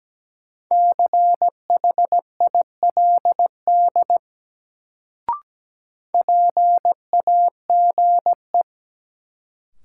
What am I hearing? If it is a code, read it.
Morse code, heard as a keyed tone, about 17 wpm, 700 Hz. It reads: CHILD PAGE
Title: Sets of 5 Words Encore - New Word Drill - 17wpm